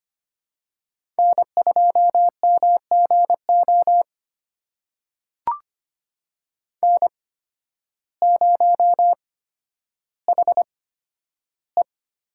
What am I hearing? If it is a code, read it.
Morse code, heard as a keyed tone, about 25 wpm, spaced as for 12 wpm, 700 Hz. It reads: N2MGO N0HE
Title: USA Call Signs - 5 Characters - 8x Character Spacing - 25wpm